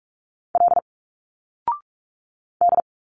Morse code, heard as a keyed tone, about 42 wpm, 700 Hz. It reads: L D
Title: Single Letters 45wpm